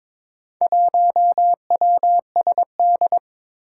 Morse code, heard as a keyed tone, about 22 wpm, 700 Hz. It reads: 1WSD
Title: Four Challenging Letter-Number Combinations 22wpm